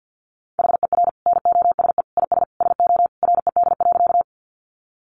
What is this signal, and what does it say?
Morse code, heard as a keyed tone, about 50 wpm, 700 Hz. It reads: HER NOSE IS SO RED.
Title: Easy Phrases 50wpm